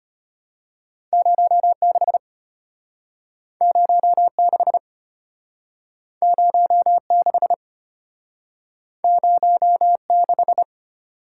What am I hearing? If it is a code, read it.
Morse code, heard as a keyed tone, about 33 wpm, 700 Hz. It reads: 06 06 06 06